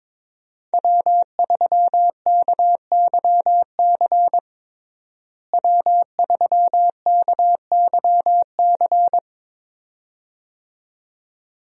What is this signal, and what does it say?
Morse code, heard as a keyed tone, about 22 wpm, 700 Hz. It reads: W3KYC W3KYC
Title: USA Call Signs - 5 Characters - Repeated 2x - 22wpm